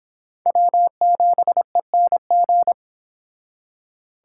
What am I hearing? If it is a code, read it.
Morse code, heard as a keyed tone, about 29 wpm, 700 Hz. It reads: W7ENG